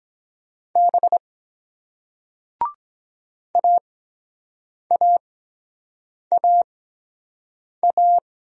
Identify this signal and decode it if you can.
Morse code, heard as a keyed tone, about 23 wpm, spaced as for 7 wpm, 700 Hz. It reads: B AAAA